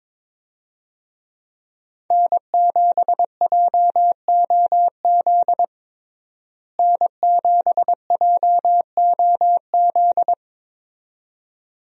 Morse code, heard as a keyed tone, about 21 wpm, 700 Hz. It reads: N7JOZ N7JOZ
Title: USA Call Signs - 5 Characters - Repeated 2x - 22wpm